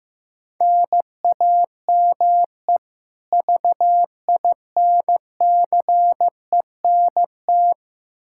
Morse code, heard as a keyed tone, about 15 wpm, 700 Hz. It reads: NAME VINCENT
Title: Practice QSO - Exchange 1 - Beginning - 15wpm